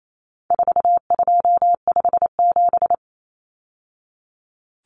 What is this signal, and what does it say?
Morse code, heard as a keyed tone, about 28 wpm, 700 Hz. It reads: 4257
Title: Numbers - 4-Digit 28wpm